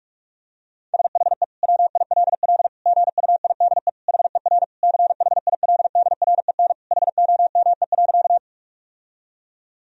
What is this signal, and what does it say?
Morse code, heard as a keyed tone, about 46 wpm, 700 Hz. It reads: SHE WILL GUIDE HER CHILDREN HOME.
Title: Sentences from Top 1000 Words - Review Part 3 - 45wpm